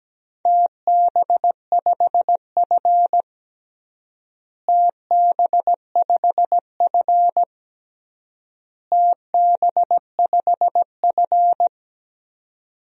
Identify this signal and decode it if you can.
Morse code, heard as a keyed tone, about 17 wpm, 700 Hz. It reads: TB5F TB5F TB5F
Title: DXCC-100 Call Signs - 4 Characters - Repeated 3x - 17wpm